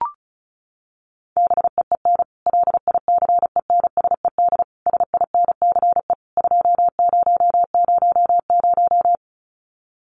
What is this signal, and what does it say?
Morse code, heard as a keyed tone, about 34 wpm, 700 Hz. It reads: BEEN LICENSED SINCE 2000
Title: Common QSO Phrases 35wpm